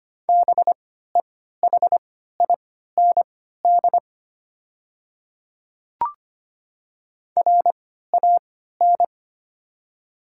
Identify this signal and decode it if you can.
Morse code, heard as a keyed tone, about 25 wpm, spaced as for 14 wpm, 690 Hz. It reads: BEHIND RAN
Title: 3x Character Spacing - Top 500 Words - 25wpm